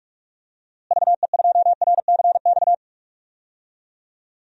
Morse code, heard as a keyed tone, about 46 wpm, 700 Hz. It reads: VE2RCX
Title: DXCC-100 Call Signs - 6 Characters - Repeated 3x - 45wpm